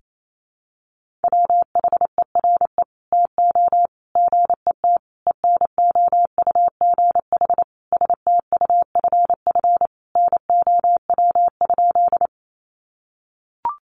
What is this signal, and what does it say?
Morse code, heard as a keyed tone, about 28 wpm, 700 Hz. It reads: WHERE TO GET ENOUGH STUFF NOW?
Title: Sentences from Top 1400 Words 28wpm